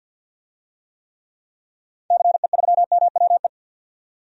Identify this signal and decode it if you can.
Morse code, heard as a keyed tone, about 45 wpm, 700 Hz. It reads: CE3MWE
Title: DXCC-100 Call Signs - 6 Characters - 50wpm